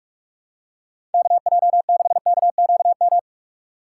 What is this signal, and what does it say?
Morse code, heard as a keyed tone, about 45 wpm, 700 Hz. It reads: KJ6KQM
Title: USA Call Signs - 4x Character Spacing - 45wpm